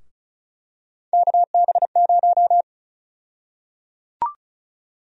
Morse code, heard as a keyed tone, about 36 wpm, 700 Hz. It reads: KB0
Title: USA Call Sign Prefixes - Repeated 2x - 35wpm